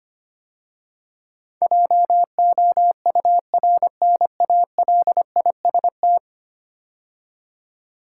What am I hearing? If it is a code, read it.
Morse code, heard as a keyed tone, about 25 wpm, 700 Hz. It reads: JOURNALIST
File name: 10 Letter Words - First 3 Letters Given - 25wpm